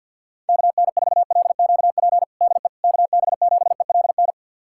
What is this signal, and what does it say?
Morse code, heard as a keyed tone, about 53 wpm, 700 Hz. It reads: KN4LQP DE KB7ELN
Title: USA Callsign DE Callsign 50wpm